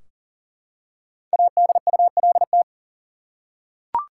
Morse code, heard as a keyed tone, about 40 wpm, 700 Hz. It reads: ADULT
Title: Five Letter Words - 40wpm